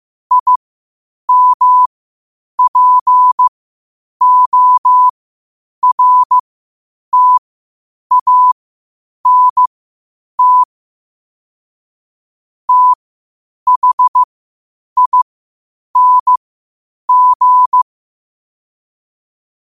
Morse code, heard as a keyed tone, about 15 wpm, spaced as for 8 wpm, 1000 Hz. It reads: IMPORTANT THING
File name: qotdfiles / 8wpm